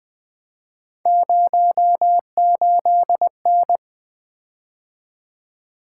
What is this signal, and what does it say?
Morse code, heard as a keyed tone, about 20 wpm, 700 Hz. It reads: 08N
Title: Three Letter-Number Combinations - 20wpm